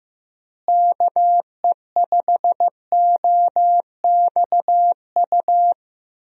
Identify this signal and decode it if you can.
Morse code, heard as a keyed tone, about 15 wpm, 700 Hz. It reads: KE5OXU
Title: USA Call Signs - 6 Characters - Repeated 3x - 15wpm